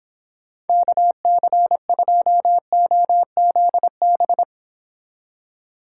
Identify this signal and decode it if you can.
Morse code, heard as a keyed tone, about 26 wpm, 700 Hz. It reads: KC2OZB